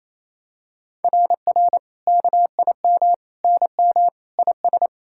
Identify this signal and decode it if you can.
Morse code, heard as a keyed tone, about 28 wpm, 700 Hz. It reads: RR KIM NM IS
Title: SOTA QSO Practice 28wpm